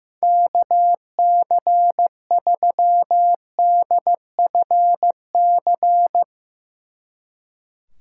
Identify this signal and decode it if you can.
Morse code, heard as a keyed tone, about 17 wpm, 700 Hz. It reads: KC3DFC
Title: Speed Racing - USA 2x3 Call Signs 15wpm